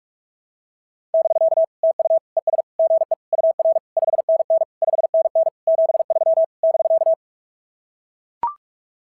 Morse code, heard as a keyed tone, about 44 wpm, 690 Hz. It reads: <BK> TU ES GE UR 5NN 5NN 73 <BK>
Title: SOTA QSO Practice 45wpm